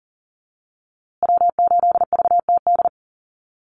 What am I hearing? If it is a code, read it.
Morse code, heard as a keyed tone, about 40 wpm, 700 Hz. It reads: W8VTD
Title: USA Call Signs Repeated 3x 40wpm